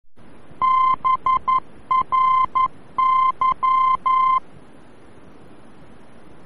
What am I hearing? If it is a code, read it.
Morse code, heard as a keyed tone, about 11 wpm, 1040 Hz. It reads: BRY
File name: BRY-VOR-F